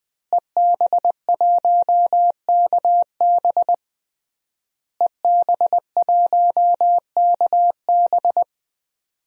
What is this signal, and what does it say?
Morse code, heard as a keyed tone, about 20 wpm, 700 Hz. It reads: EB1KB EB1KB
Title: DXCC-100 Call Signs - 5 Characters - Repeated 3x - 20wpm